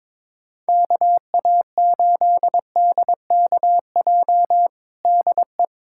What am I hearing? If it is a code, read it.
Morse code, heard as a keyed tone, about 22 wpm, 700 Hz. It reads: KA8DKJ DE
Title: USA Callsign DE Callsign - 6 Characters - 22wpm